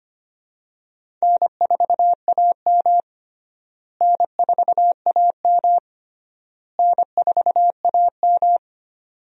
Morse code, heard as a keyed tone, about 25 wpm, 700 Hz. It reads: N4AM N4AM N4AM
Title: USA Call Signs - 4 Characters - Repeated 3x - 25wpm